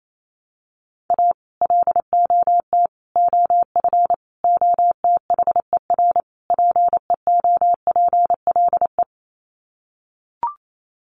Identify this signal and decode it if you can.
Morse code, heard as a keyed tone, about 28 wpm, 700 Hz. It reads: A LOT OF OTHER PEOPLE
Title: Sets of 5 Words Encore2 28wpm